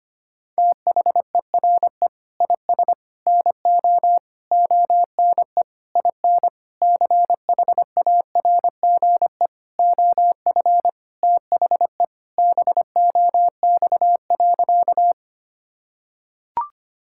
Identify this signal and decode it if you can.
Morse code, heard as a keyed tone, about 25 wpm, 700 Hz. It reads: THERE IS NO ONE IN CHARGE OF THE BOX.
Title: Sentences from Top 1000 Words - Review Part 6 - 25wpm